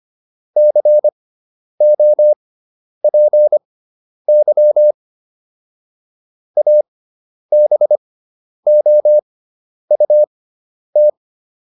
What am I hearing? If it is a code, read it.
Morse code, heard as a keyed tone, about 25 wpm, spaced as for 10 wpm, 600 Hz. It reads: COPY ABOUT